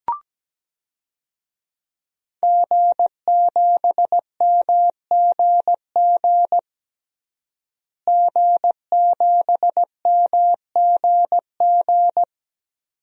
Morse code, heard as a keyed tone, about 17 wpm, 700 Hz. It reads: G7MGG G7MGG
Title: DXCC-100 Call Signs - 5 Characters - Repeated 2x - 17wpm